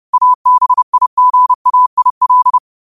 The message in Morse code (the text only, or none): ABIGAIL